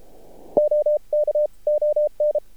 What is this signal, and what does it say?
Morse code, heard as a keyed tone, about 33 wpm, 600 Hz. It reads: OK ON
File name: archived / CRSnet